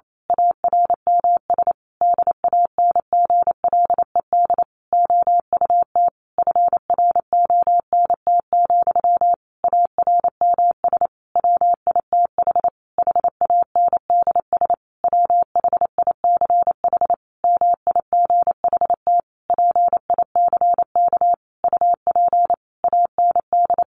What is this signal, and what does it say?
Morse code, heard as a keyed tone, about 28 wpm, 700 Hz. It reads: ARMS DANGLED OUT FRONT, ARMS WITH HANDS WHICH MIGHT PICK UP AND
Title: A Sound Of Thunder 28wpm